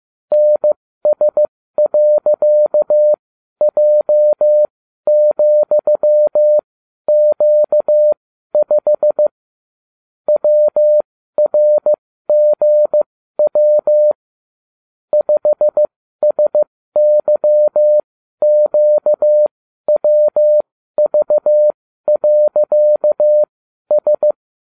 Morse code, heard as a keyed tone, about 15 wpm, spaced as for 12 wpm, 600 Hz. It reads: NS.J,Q5 WRGW 5SYQWV.S